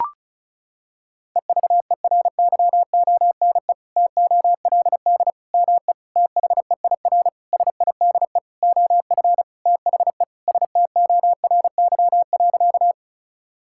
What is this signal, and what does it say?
Morse code, heard as a keyed tone, about 36 wpm, 700 Hz. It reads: EVERYONE TOLD ME THEIR SIDE OF THE STORY.
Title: Sentences from Top 1000 Words - Review Part 5 - 35wpm